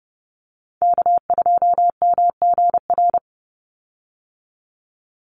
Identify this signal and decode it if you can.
Morse code, heard as a keyed tone, about 28 wpm, 700 Hz. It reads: K2MGR
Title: Call Signs - 30wpm